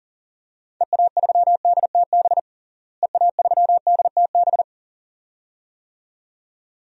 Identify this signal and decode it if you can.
Morse code, heard as a keyed tone, about 37 wpm, 700 Hz. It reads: EA3DTB EA3DTB
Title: DXCC-100 Call Signs - 6 Characters - Repeated 2x - 40wpm